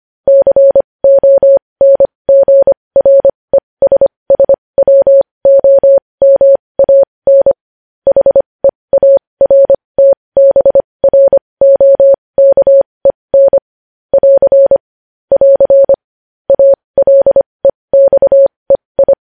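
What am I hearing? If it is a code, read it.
Morse code, heard as a keyed tone, about 25 wpm, spaced as for 20 wpm, 550 Hz. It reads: CONGRESSWOMAN HEARTBROKEN <AR> <AR> ALEXEI
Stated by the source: Morse code news headlines.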